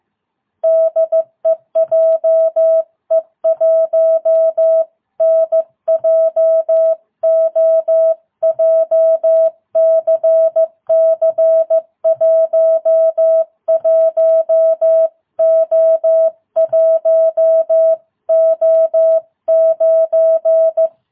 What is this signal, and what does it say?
Morse code, heard as a keyed tone, about 15 wpm, 660 Hz. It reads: DE JE1NJO JCC 11O1O9